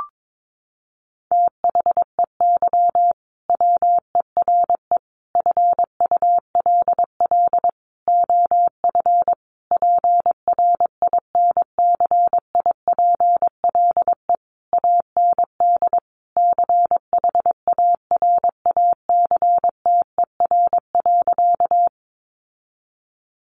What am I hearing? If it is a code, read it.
Morse code, heard as a keyed tone, about 22 wpm, 700 Hz. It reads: THEY WERE FULL OF PRINCIPLE AND CHARACTER.
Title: Sentences from Top 1400 Words 22wpm